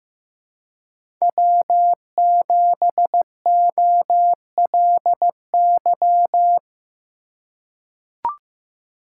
Morse code, heard as a keyed tone, about 15 wpm, 700 Hz. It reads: W7OLY
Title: USA Call Signs - 3x Character Spacing - 15wpm